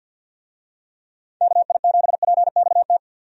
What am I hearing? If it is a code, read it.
Morse code, heard as a keyed tone, about 50 wpm, 700 Hz. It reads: KI7PXT
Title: USA Call Signs - 6 Characters - 50wpm